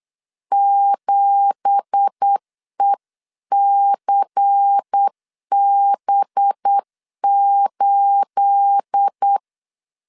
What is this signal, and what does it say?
Morse code, heard as a keyed tone, about 8 wpm, 800 Hz. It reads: 7ECB8
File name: CW / DET / 1 9 1500C S C 8wpm 5653